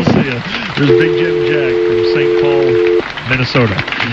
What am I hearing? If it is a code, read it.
Dial tone (North America), 350 + 440 Hz, with speech.
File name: big-jim-jack